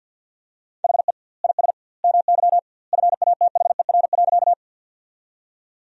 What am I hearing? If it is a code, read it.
Morse code, heard as a keyed tone, about 52 wpm, 700 Hz. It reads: HE IS MY FATHER.
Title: 2x Word-Spacing - Sentences from Top 300 Words - 50wpm